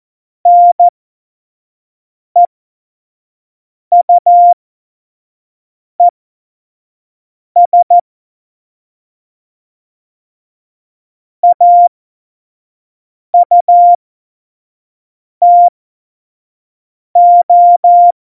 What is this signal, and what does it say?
Morse code, heard as a keyed tone, about 14 wpm, spaced as for 5 wpm, 700 Hz. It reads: NEUES AUTO
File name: 25 F70